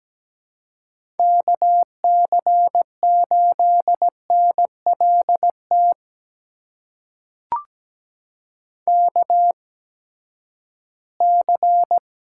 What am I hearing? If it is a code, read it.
Morse code, heard as a keyed tone, about 17 wpm, spaced as for 12 wpm, 700 Hz. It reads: KC8NLT KC
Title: USA Call Signs - 6 Characters - 8x Character Spacing - 17wpm